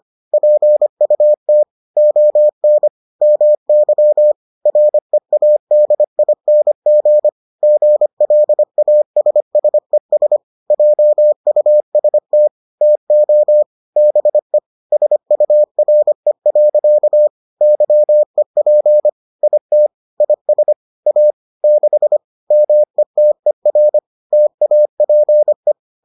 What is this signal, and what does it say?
Morse code, heard as a keyed tone, about 25 wpm, 600 Hz. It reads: PUT ON MY READING GLASSES JUST TO BE SURE. YEP IT IS A 6 METER TAPE